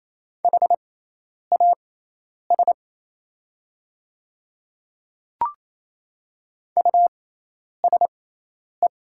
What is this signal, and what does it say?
Morse code, heard as a keyed tone, about 27 wpm, spaced as for 10 wpm, 700 Hz. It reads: HAS USE
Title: 6x Character Spacing - Top 100 Words - 28wpm